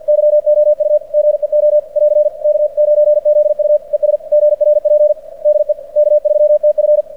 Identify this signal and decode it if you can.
Morse code, heard as a keyed tone, about 46 wpm, 600 Hz. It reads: YOU GEO <AR> K9ZU ES GRP DE K3TF
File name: archived / CRSnet